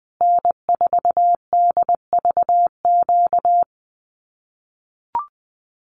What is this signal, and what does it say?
Morse code, heard as a keyed tone, about 19 wpm, 700 Hz. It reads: N4DVQ
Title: Common QSO Phrases Encore 20wpm